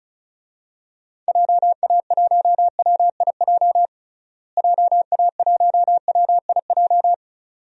Morse code, heard as a keyed tone, about 36 wpm, 700 Hz. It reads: JA1WIJ JA1WIJ
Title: DXCC-100 Call Signs - 6 Characters - Repeated 2x - 35wpm